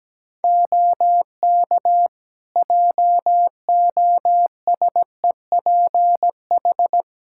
Morse code, heard as a keyed tone, about 17 wpm, 700 Hz. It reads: OK JOSEPH
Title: Practice QSO - Exchange 3 - More Info - 17wpm